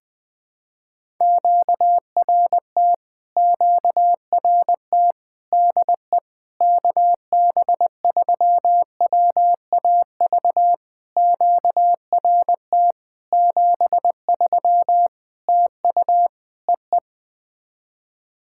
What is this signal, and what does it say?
Morse code, heard as a keyed tone, about 20 wpm, 700 Hz. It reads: QRT QRT DE KB3WAV QRT 73 TU EE
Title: POTA QSO Practice - 20wpm